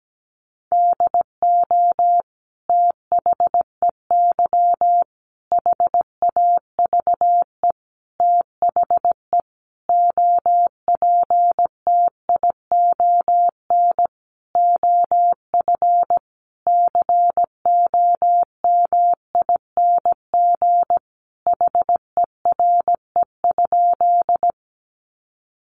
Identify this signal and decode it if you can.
Morse code, heard as a keyed tone, about 17 wpm, 700 Hz. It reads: DO THEY HAVE THE OPTION OF COMING HERE?
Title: Sentences from Top 1300 Words 17wpm